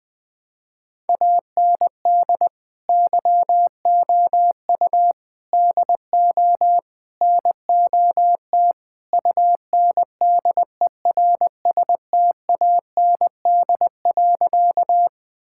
Morse code, heard as a keyed tone, about 20 wpm, 700 Hz. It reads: AND YOU DO NOT UNDERSTAND.
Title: Sentences with 4-Word Phrases - 20wpm